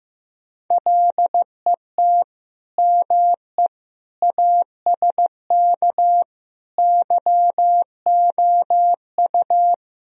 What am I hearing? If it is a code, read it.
Morse code, heard as a keyed tone, about 16 wpm, 700 Hz. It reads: LET ME ASK YOU
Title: Speed-Racing - Sets of 5 Words 15wpm